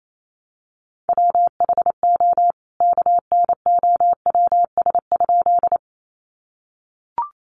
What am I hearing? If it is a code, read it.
Morse code, heard as a keyed tone, about 28 wpm, 700 Hz. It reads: WHO KNOWS?
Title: A Sound Of Thunder 28wpm